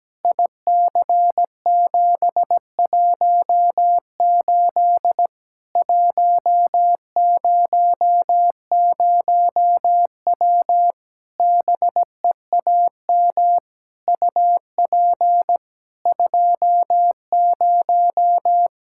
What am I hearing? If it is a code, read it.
Morse code, heard as a keyed tone, about 17 wpm, 700 Hz. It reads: IC718 100W BEAM UP 20
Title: K1USN SST QSO Practice 17wpm